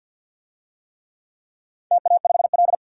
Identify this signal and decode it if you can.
Morse code, heard as a keyed tone, about 52 wpm, 700 Hz. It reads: TA5L
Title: DXCC-100 Call Signs - 4 Characters - 50wpm